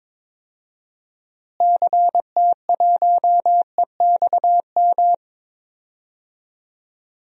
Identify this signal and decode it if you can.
Morse code, heard as a keyed tone, about 22 wpm, 700 Hz. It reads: CT1EXM
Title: DXCC-100 Call Signs - 6 Characters - Repeated 2x - 22wpm